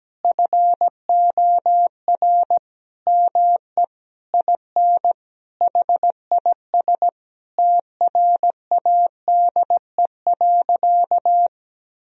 Morse code, heard as a keyed tone, about 17 wpm, 700 Hz. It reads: FOR ME IN HIS TRADE.
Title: Sentences from Top 1000 Words - Review Part 4 - 17wpm